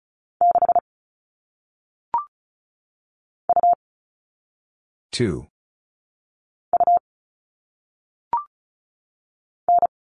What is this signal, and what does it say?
Morse code, heard as a keyed tone, about 35 wpm, 700 Hz. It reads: 6 U U N
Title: All Cut Numbers - 1-Digit 35wpm